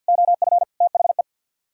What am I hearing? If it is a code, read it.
Morse code, heard as a keyed tone, about 64 wpm, 700 Hz. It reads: OF THE